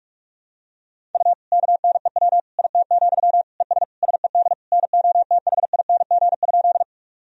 Morse code, heard as a keyed tone, about 46 wpm, 700 Hz. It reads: U KNEW IT, ES SED NOTHING?
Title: Sentences from Top 500 Words and Abbreviations 45wpm